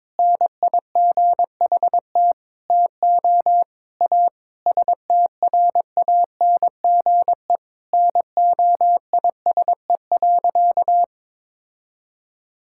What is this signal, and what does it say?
Morse code, heard as a keyed tone, about 22 wpm, 700 Hz. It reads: NIGHT TO A STRANGE NOISE.
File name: practice / 22wpm / Sentences with 8-Word Phrases - 22wpm